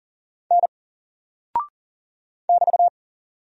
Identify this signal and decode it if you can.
Morse code, heard as a keyed tone, about 40 wpm, 700 Hz. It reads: N <BT>
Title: Single Character (Letter:Number:Punctuation:Prosign) 40wpm